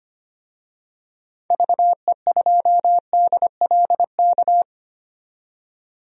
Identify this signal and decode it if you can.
Morse code, heard as a keyed tone, about 25 wpm, 700 Hz. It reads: VE2DLK
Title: DXCC-100 Call Signs - 6 Characters - 25wpm